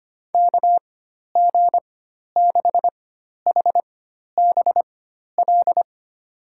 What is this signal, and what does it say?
Morse code, heard as a keyed tone, about 25 wpm, spaced as for 12 wpm, 700 Hz. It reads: KG6HBL
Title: USA Call Signs - 4x Character Spacing - 25wpm